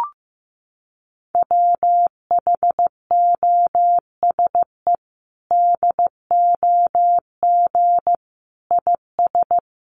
Morse code, heard as a keyed tone, about 15 wpm, 700 Hz. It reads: WHOSE DOG IS
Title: Sentences from Top 1200 Words 15wpm